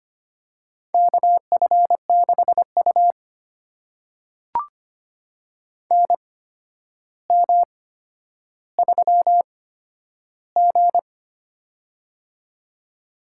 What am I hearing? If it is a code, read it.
Morse code, heard as a keyed tone, about 25 wpm, spaced as for 11 wpm, 700 Hz. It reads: KF6U NM3G
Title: USA Call Signs - 4 Characters - 8x Character Spacing - 25wpm